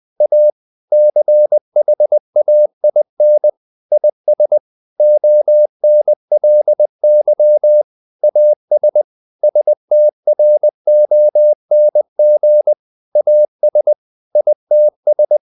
Morse code, heard as a keyed tone, about 20 wpm, 600 Hz. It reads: A CHAIN IS ONLY AS STRONG AS ITS
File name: CW / Practice / EnglishSayings20